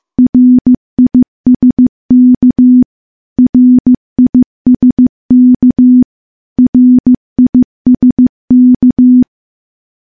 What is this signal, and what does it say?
Morse code, heard as a keyed tone, about 15 wpm, 250 Hz. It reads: RISK RISK RISK
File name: morse / morse-training-15WPM